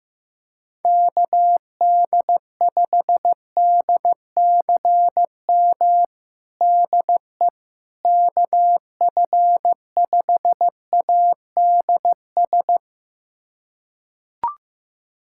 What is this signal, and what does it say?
Morse code, heard as a keyed tone, about 15 wpm, 700 Hz. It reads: KD5DCM DE KF5ADS
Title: USA Callsign DE Callsign 15wpm